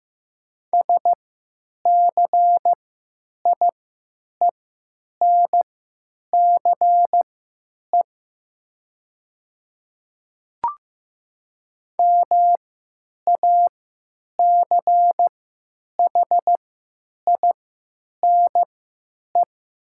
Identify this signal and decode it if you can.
Morse code, heard as a keyed tone, about 15 wpm, spaced as for 9 wpm, 700 Hz. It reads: SCIENCE MACHINE
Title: 3x Character Spacing - Top 500 Words - 15wpm